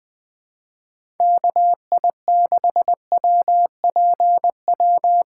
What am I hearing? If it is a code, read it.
Morse code, heard as a keyed tone, about 20 wpm, 700 Hz. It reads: KI6WPW
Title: USA Call Signs - 6 Characters - 3x Character Spacing - 20wpm